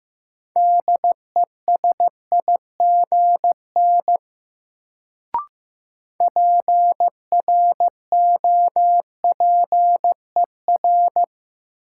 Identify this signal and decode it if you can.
Morse code, heard as a keyed tone, about 15 wpm, 700 Hz. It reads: DESIGN PROPER
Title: Sentences from Top 800 Words - New Word Drill - 15wpm